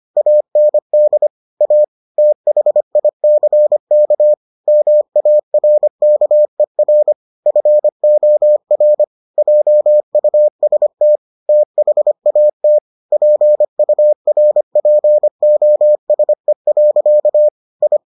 Morse code, heard as a keyed tone, about 25 wpm, 600 Hz. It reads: AND A THICK MARKER FOR JUST THAT PURPOSE. I